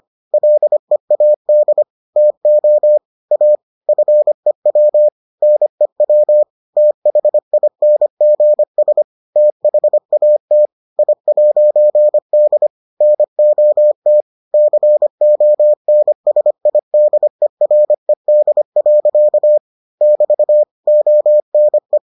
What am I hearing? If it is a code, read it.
Morse code, heard as a keyed tone, about 25 wpm, 600 Hz. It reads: LEAD TO A FEW NEW THINGS THAT I'D NOT CONSIDERED. <BT> ONE